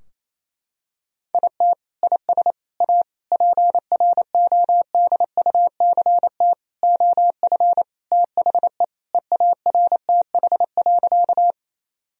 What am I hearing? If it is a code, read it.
Morse code, heard as a keyed tone, about 27 wpm, 700 Hz. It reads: IT IS A PRODUCT OF THE EARTH.
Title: Sentences from Top 1000 Words - Review Part 2 - 28wpm